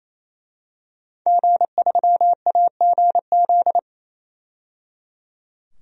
Morse code, heard as a keyed tone, about 28 wpm, 700 Hz. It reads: G3AGZ
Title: DXCC-100 Call Signs - 5 Characters - 28wpm